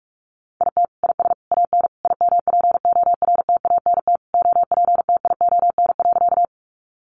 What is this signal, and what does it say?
Morse code, heard as a keyed tone, about 44 wpm, 700 Hz. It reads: IT IS AN IMPORTANT OPTION.
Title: Sentences from Top 1300 Words 45wpm